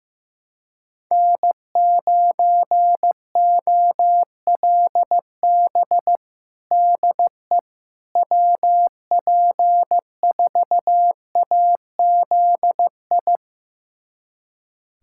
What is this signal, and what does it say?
Morse code, heard as a keyed tone, about 15 wpm, 700 Hz. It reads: N9OLB DE WP4AZI
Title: USA Callsign DE Callsign 15wpm